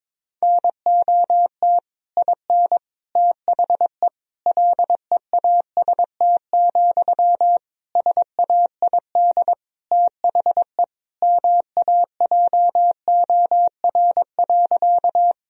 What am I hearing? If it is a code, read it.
Morse code, heard as a keyed tone, about 21 wpm, 700 Hz. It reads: NOT IN THE LEAST, SAID THE MAJOR.
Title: Sentences from Top 1000 Words - Review Part 4 - 22wpm